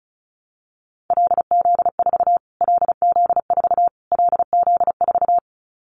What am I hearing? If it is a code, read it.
Morse code, heard as a keyed tone, about 34 wpm, 700 Hz. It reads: LZ4 LZ4 LZ4
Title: Callsign Prefixes for Top 25 DXCC - Repeated 3x - 35wpm